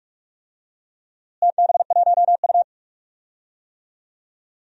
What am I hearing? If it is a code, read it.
Morse code, heard as a keyed tone, about 41 wpm, 700 Hz. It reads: TB1U
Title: DXCC-100 Call Signs - 4 Characters - Repeated 2x - 45wpm